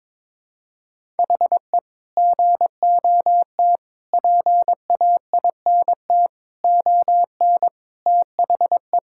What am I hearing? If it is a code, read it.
Morse code, heard as a keyed tone, about 22 wpm, 700 Hz. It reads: HE GOT PAINT ON THE
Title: Sentences from Top 1000 Words - Review Part 6 - 22wpm